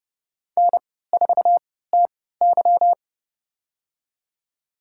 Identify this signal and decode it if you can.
Morse code, heard as a keyed tone, about 30 wpm, spaced as for 17 wpm, 700 Hz. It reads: N4TY
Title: USA Call Signs - 4 Characters - 3x Character Spacing - 30wpm